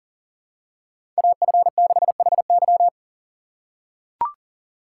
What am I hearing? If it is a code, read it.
Morse code, heard as a keyed tone, about 40 wpm, 700 Hz. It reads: AF6HY
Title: USA Call Signs - 5 Characters - 2x Character Spacing - 40wpm